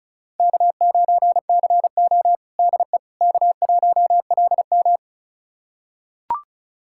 Morse code, heard as a keyed tone, about 36 wpm, 700 Hz. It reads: K9CO DE K1LM
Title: USA Callsign DE Callsign - 4 Characters - 35wpm